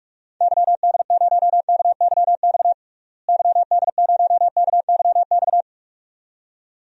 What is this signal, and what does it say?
Morse code, heard as a keyed tone, about 41 wpm, 700 Hz. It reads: YD0KYX YD0KYX
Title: DXCC-100 Call Signs - 6 Characters - Repeated 2x - 45wpm